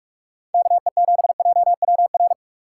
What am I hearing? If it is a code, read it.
Morse code, heard as a keyed tone, about 59 wpm, spaced as for 51 wpm, 700 Hz. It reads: KE7JWR
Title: Speed Racing - USA Call Signs 45wpm